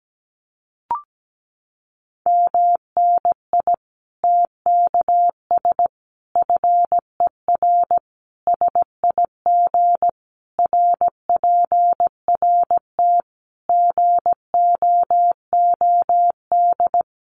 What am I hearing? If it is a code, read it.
Morse code, heard as a keyed tone, about 17 wpm, 700 Hz. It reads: MNI TKS FER SIG RPRT GOOD
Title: Common QSO Phrases 17wpm